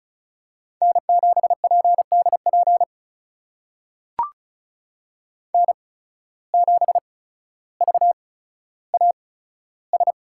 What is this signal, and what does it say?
Morse code, heard as a keyed tone, about 36 wpm, spaced as for 15 wpm, 700 Hz. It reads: N7PDP N7VAS
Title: USA Call Signs - 5 Characters - 8x Character Spacing - 35wpm